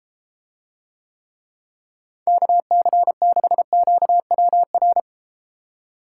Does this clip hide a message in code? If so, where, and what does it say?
Yes: Morse code from 2.3 to 5.0 s, heard as a keyed tone, about 32 wpm, 700 Hz. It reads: KC6QWR